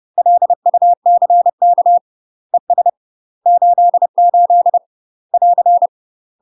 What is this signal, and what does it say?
Morse code, heard as a keyed tone, about 30 wpm, 700 Hz. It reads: LUCK ES 88 <AR>
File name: UK MASTER QSOs TRAINING / 30WPM / UK MASTER QSOs 0009